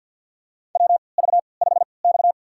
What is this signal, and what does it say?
Morse code, heard as a keyed tone, about 47 wpm, spaced as for 28 wpm, 700 Hz. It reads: WV5X
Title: USA Call Signs - 4 Characters - 3x Character Spacing - 50wpm